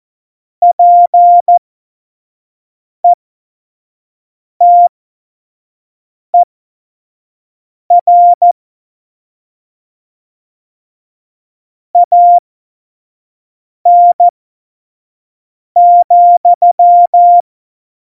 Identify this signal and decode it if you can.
Morse code, heard as a keyed tone, about 14 wpm, spaced as for 5 wpm, 700 Hz. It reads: PETER AN,
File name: Morsekurs / Witze / Witz 26. Februar 2019, 25 F70